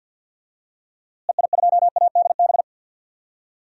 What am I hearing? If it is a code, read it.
Morse code, heard as a keyed tone, about 46 wpm, 700 Hz. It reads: EI2ADB
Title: DXCC-100 Call Signs - 6 Characters - Repeated 3x - 50wpm